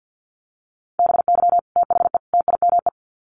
Morse code, heard as a keyed tone, about 48 wpm, 700 Hz. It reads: BY THE TIME
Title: Sets of 3 Words 50wpm